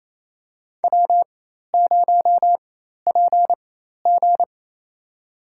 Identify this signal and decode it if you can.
Morse code, heard as a keyed tone, about 28 wpm, spaced as for 13 wpm, 700 Hz. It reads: W0PG
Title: USA Call Signs - 4 Characters - 4x Character Spacing - 28wpm